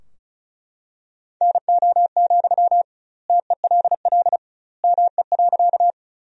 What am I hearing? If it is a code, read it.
Morse code, heard as a keyed tone, about 36 wpm, 700 Hz. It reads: NO, TELL ME.
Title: 2x Word-Spacing - Sentences from Top 200 Words - 35wpm